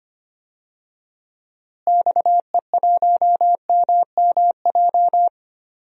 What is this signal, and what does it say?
Morse code, heard as a keyed tone, about 25 wpm, 700 Hz. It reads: XE1MMJ
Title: DXCC-100 Call Signs - 6 Characters - Repeated 2x - 25wpm